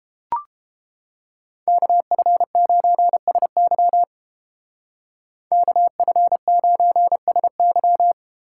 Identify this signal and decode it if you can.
Morse code, heard as a keyed tone, about 32 wpm, 700 Hz. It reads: KF9SY KF9SY